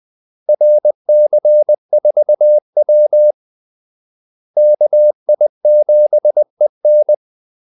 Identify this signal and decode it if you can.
Morse code, heard as a keyed tone, about 20 wpm, 600 Hz. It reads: RC4W KI7EN
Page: Kaldesignaler 20 wpm